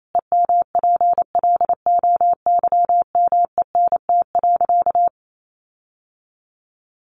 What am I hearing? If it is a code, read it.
Morse code, heard as a keyed tone, about 28 wpm, 700 Hz. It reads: EMPLOYMENT.
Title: Sentences from Top 1500 Words 28wpm